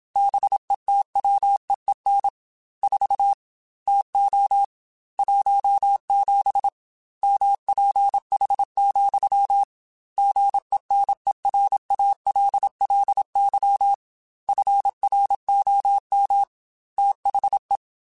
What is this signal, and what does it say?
Morse code, heard as a keyed tone, about 27 wpm, 800 Hz. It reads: BETWEEN 4 TO 17 MPH, GENERALLY FROM THE
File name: Weather Sets / Weather Sets-mp3s / Weather 4 21 @25